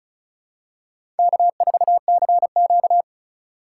Morse code, heard as a keyed tone, about 36 wpm, 700 Hz. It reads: K4CQ
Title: USA Call Signs - 4 Characters - 2x Character Spacing - 35wpm